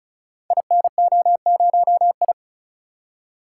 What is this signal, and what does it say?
Morse code, heard as a keyed tone, about 35 wpm, 700 Hz. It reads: INO0I
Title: Code Groups - 5 Letter:Numbers - 35wpm